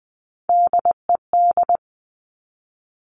Morse code, heard as a keyed tone, about 20 wpm, 700 Hz. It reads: DED
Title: Top Three Letter Combinations in Words 20wpm